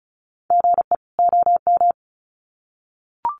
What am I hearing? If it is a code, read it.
Morse code, heard as a keyed tone, about 34 wpm, 700 Hz. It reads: GE OM
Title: Common QSO Phrases 35wpm